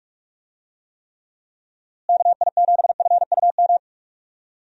Morse code, heard as a keyed tone, about 41 wpm, 700 Hz. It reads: KI7FUM
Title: USA Call Signs - 6 Characters - Repeated 2x - 45wpm